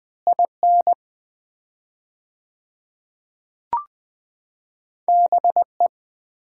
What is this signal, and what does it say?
Morse code, heard as a keyed tone, about 20 wpm, 700 Hz. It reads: IN BE
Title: 4x Word-Spacing - Sets of 5 Words - 20wpm